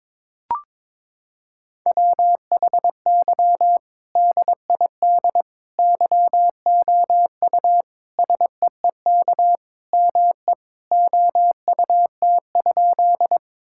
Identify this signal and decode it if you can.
Morse code, heard as a keyed tone, about 22 wpm, 700 Hz. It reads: WHY DID YOU SEEK ME OUT?
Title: Sentences from Top 1600 Words 22wpm